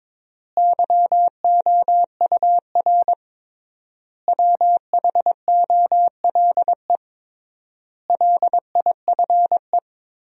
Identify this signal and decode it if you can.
Morse code, heard as a keyed tone, about 22 wpm, 700 Hz. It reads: YOUR WHOLE LIFE
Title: 3x Word-Spacing - Sentences from Top 400 Words - 22wpm